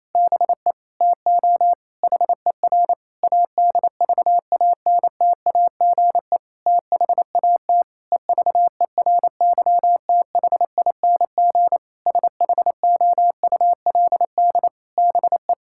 Morse code, heard as a keyed tone, about 28 wpm, 700 Hz. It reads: BE TO HER ADVANTAGE THAT EVERYTHING SHOULD BE
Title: Sentences from Top 1000 Words - Review Part 6 - 28wpm